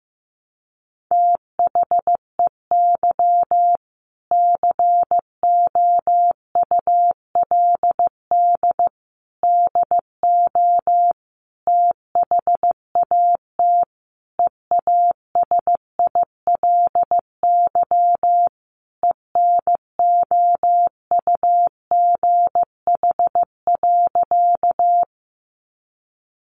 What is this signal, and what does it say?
Morse code, heard as a keyed tone, about 15 wpm, 700 Hz. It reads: THEY COULD DO THAT EASILY ENOUGH.
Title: Sentences from Top 900 Words 15wpm